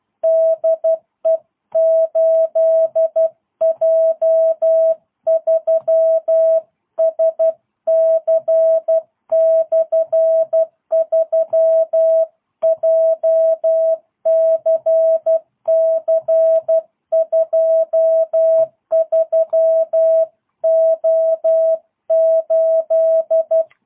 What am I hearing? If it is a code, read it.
Morse code, heard as a keyed tone, about 12 wpm, 650 Hz. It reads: DE 8J3SC/3 JCC23O8
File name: jh8cbh / 2022cw